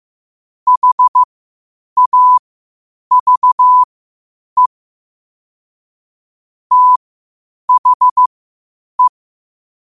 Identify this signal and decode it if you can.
Morse code, heard as a keyed tone, about 15 wpm, spaced as for 8 wpm, 1000 Hz. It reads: HAVE THE
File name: qotdfiles / 8wpm